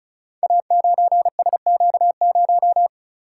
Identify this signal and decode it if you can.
Morse code, heard as a keyed tone, about 36 wpm, 690 Hz. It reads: A9SQ0
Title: Code Groups - 5 Letter:Numbers - 35wpm